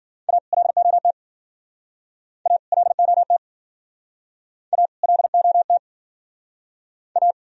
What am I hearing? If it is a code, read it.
Morse code, heard as a keyed tone, about 54 wpm, 700 Hz. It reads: A LOT A LOT A LOT A